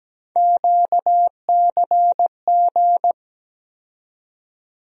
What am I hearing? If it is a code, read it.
Morse code, heard as a keyed tone, about 17 wpm, 700 Hz. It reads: QCG
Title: USA Call Sign Postfixes - 3 Letters 17wpm